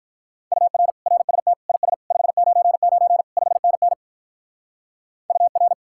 Morse code, heard as a keyed tone, about 44 wpm, 700 Hz. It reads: UR RST IS 599 5NN UR